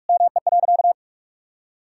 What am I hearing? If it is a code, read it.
Morse code, heard as a keyed tone, about 46 wpm, 700 Hz. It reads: ME.
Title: Sentences from Top 1000 Words - Review Part 5 - 45wpm